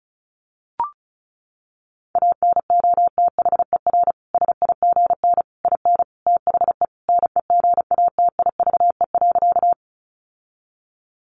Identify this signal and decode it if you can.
Morse code, heard as a keyed tone, about 34 wpm, 700 Hz. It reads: ANOTHER SIGN IN THE NEGATIVE.
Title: Sentences from Top 1000 Words 35wpm